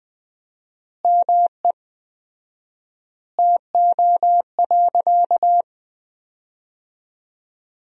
Morse code, heard as a keyed tone, about 20 wpm, 700 Hz. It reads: ME TO.
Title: 4x Word-Spacing - Sentences from Top 400 Words - 20wpm